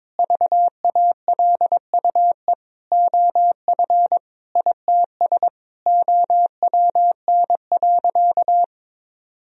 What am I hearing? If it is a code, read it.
Morse code, heard as a keyed tone, about 22 wpm, 700 Hz. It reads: VALUE OF ITS OWN.
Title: Sentences from Top 1000 Words - Review Part 4 - 22wpm